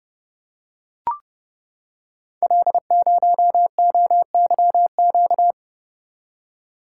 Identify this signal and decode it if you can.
Morse code, heard as a keyed tone, about 30 wpm, 700 Hz. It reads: L0OYQ
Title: Code Groups - 5 Letter:Numbers - 30wpm